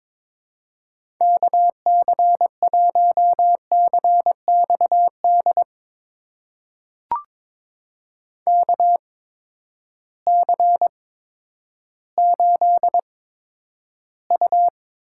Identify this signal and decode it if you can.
Morse code, heard as a keyed tone, about 22 wpm, spaced as for 11 wpm, 700 Hz. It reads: KC1CXD KC8U
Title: USA Call Signs - 6 Characters - 8x Character Spacing - 22wpm